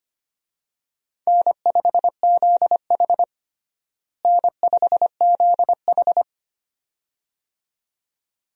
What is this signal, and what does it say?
Morse code, heard as a keyed tone, about 25 wpm, 700 Hz. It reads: N5ZH N5ZH
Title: USA Call Signs - 4 Characters - Repeated 2x - 25wpm